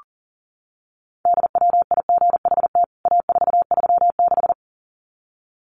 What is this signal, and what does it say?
Morse code, heard as a keyed tone, about 40 wpm, 700 Hz. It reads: DWIGHT A436
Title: CWT Exchange 40wpm